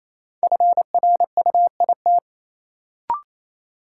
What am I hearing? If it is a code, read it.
Morse code, heard as a keyed tone, about 27 wpm, 700 Hz. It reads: FRUIT
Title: Sentences from Top 1300 Words - New Word Drill Alt - 28wpm